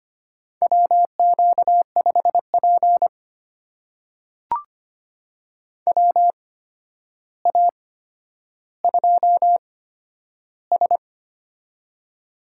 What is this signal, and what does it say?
Morse code, heard as a keyed tone, about 25 wpm, spaced as for 11 wpm, 700 Hz. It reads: WQ5P WA2S
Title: USA Call Signs - 4 Characters - 8x Character Spacing - 25wpm